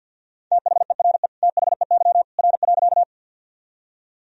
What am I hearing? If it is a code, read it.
Morse code, heard as a keyed tone, about 50 wpm, 700 Hz. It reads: THERE THEY R.
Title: Sentences from Top 100 Words and Abbreviations 50wpm